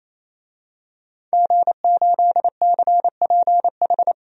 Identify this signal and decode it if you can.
Morse code, heard as a keyed tone, about 28 wpm, 700 Hz. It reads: G8CPH
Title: DXCC-100 Call Signs - 5 Characters - 28wpm